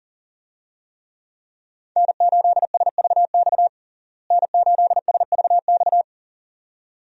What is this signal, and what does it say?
Morse code, heard as a keyed tone, about 40 wpm, 700 Hz. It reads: N8SVX N8SVX
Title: USA Call Signs - 5 Characters - Repeated 2x - 40wpm